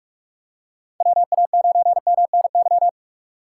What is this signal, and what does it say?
Morse code, heard as a keyed tone, about 44 wpm, 700 Hz. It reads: WA9MNY